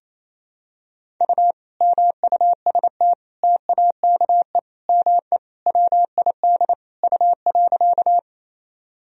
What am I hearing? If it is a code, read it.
Morse code, heard as a keyed tone, about 28 wpm, 700 Hz. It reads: U MUST TAKE ME WID U.
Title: Sentences from Top 200 Words and Abbreviations 28wpm